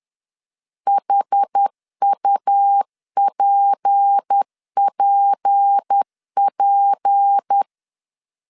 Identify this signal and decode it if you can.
Morse code, heard as a keyed tone, about 10 wpm, 800 Hz. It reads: HUPPP
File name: LEZIONI / 6 PKX0 / 6 P 10wpm 2432